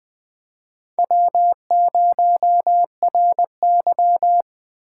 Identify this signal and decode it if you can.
Morse code, heard as a keyed tone, about 20 wpm, 700 Hz. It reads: W0RY
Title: USA Call Signs - 3x Character Spacing - 20wpm